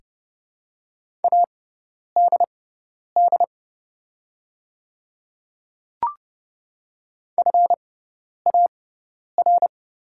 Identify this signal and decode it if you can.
Morse code, heard as a keyed tone, about 29 wpm, spaced as for 10 wpm, 700 Hz. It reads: ADD FAR
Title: 6x Character Spacing - Top 300 Words - 30wpm